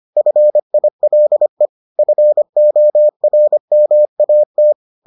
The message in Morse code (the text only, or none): FILE FORMAT